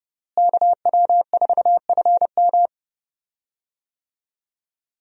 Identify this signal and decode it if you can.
Morse code, heard as a keyed tone, about 30 wpm, 700 Hz. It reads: KW4FM
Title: USA Call Signs - 5 Characters - Repeated 3x - 30wpm